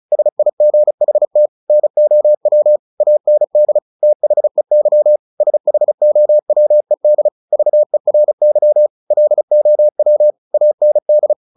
Morse code, heard as a keyed tone, about 34 wpm, 600 Hz. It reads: SIGHT NOW AND THEY SHOWED VERY LOW AND
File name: the old man and the sea / the old man and the sea 1 35wpm